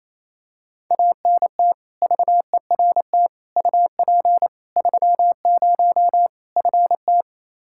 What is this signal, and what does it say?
Morse code, heard as a keyed tone, about 28 wpm, 700 Hz. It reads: ANT VERT UP 30 FT
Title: K1USN SST QSO Practice 28wpm